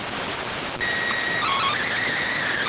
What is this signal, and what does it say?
SSTV (slow-scan television), PD 290.